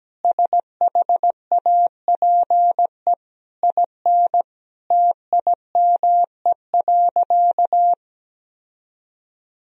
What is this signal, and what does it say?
Morse code, heard as a keyed tone, about 17 wpm, 700 Hz. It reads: SHAPE IN TIME.
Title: Sentences from Top 1000 Words - Review Part 3 - 17wpm